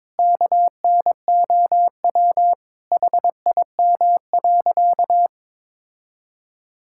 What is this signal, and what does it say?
Morse code, heard as a keyed tone, about 22 wpm, 700 Hz. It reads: KNOW HIM.
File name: practice / 22wpm / Sentences with 4-Word Phrases - 22wpm